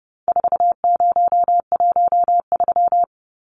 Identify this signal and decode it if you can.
Morse code, heard as a keyed tone, about 30 wpm, 700 Hz. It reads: 4013
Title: Numbers - 4-Digit 30wpm